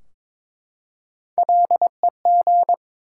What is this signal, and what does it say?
Morse code, heard as a keyed tone, about 22 wpm, 700 Hz. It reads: LEG
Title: Three Letter Words - 22wpm